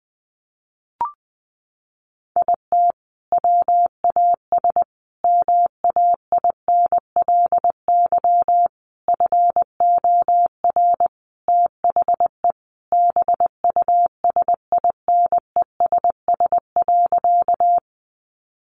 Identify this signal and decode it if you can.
Morse code, heard as a keyed tone, about 20 wpm, 700 Hz. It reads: IT WAS MAINLY FOR THE BUSINESS.
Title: Sentences from Top 1400 Words 20wpm